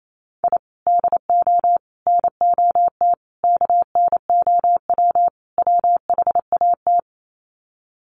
Morse code, heard as a keyed tone, about 28 wpm, 700 Hz. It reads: I DO NOT KNOW WHAT
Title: Sets of 5 Words 28wpm